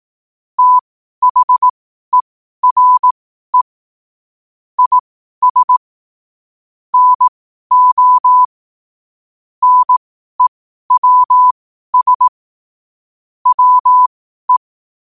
Morse code, heard as a keyed tone, about 18 wpm, spaced as for 12 wpm, 1000 Hz. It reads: THERE IS NO NEWS WE
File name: qotdfiles / 12wpm